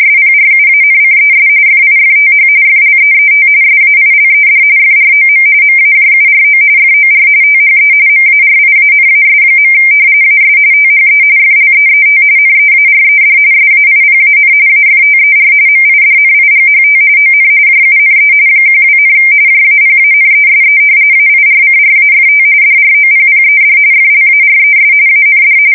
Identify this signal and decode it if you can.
RTTY, about 45.45 baud. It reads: SPRUANCES FIFTH FLEET SAILED FROM MAJURO FOR THE MARIANAS. AS THE FLEET NEARED SAIPAN AFTER MIDNIGHT ON JUNE 12, JAPANESE AIRCRAFT DROPPED FLARES AND
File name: fsk170 / battleshipnj